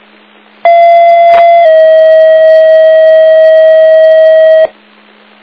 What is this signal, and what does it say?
Two-tone page, heard as a one-second tone, then a three-second tone, about 680 then 640 Hz. It is heard from 0.6 to 4.7 s.
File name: Big Moose